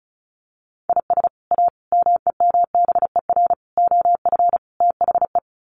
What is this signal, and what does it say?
Morse code, heard as a keyed tone, about 34 wpm, 700 Hz. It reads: IS A MEMBER OF THE
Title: Sets of 5 Words 35wpm